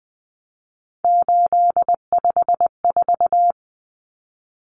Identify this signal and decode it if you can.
Morse code, heard as a keyed tone, about 20 wpm, 700 Hz. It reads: 854
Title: Common Cut Numbers - 3-Digit 20wpm